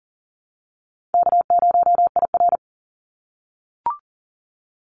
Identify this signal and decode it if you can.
Morse code, heard as a keyed tone, about 40 wpm, 700 Hz. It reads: K0IR
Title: Easy Call Signs - 40wpm